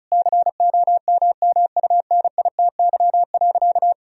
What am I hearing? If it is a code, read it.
Morse code, heard as a keyed tone, about 34 wpm, 700 Hz. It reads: COMMUNITY.
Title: Sentences from Top 1000 Words - Review Part 4 - 35wpm